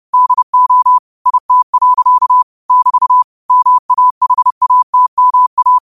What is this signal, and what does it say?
Morse code, heard as a keyed tone, about 30 wpm, 1000 Hz. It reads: DO IT. <BT> MAHATMA
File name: qotdfiles / 30wpm